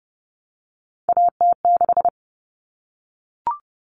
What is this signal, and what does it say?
Morse code, heard as a keyed tone, about 30 wpm, 700 Hz. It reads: AT6
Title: Common Cut Numbers - 3-Digit 30wpm